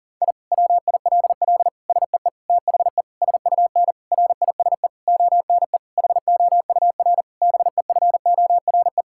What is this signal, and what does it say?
Morse code, heard as a keyed tone, about 40 wpm, 700 Hz. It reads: I WILL SEE THE SUN RISE ONE HOUR BEFORE
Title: Sentences from Top 1000 Words - Review Part 6 - 40wpm